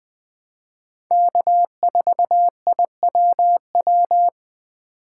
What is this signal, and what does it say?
Morse code, heard as a keyed tone, about 20 wpm, 700 Hz. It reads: K4IWW
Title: USA Call Signs - 5 Characters - 4x Character Spacing - 20wpm